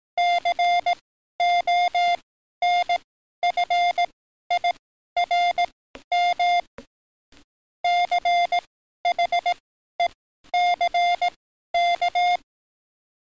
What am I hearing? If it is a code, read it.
Morse code, heard as a keyed tone, about 18 wpm, spaced as for 12 wpm, 700 Hz. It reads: CONFIRM CHECK
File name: training / NAVPERS 11049to11060 / NAVPERS 11053 RA RB